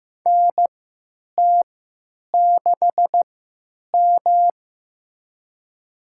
Morse code, heard as a keyed tone, about 15 wpm, spaced as for 8 wpm, 700 Hz. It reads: NT6M
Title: USA Call Signs - 3x Character Spacing - 15wpm